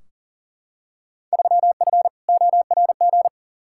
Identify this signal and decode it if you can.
Morse code, heard as a keyed tone, about 40 wpm, 700 Hz. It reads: 3F ORG
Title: Field Day Exchanges - 40wpm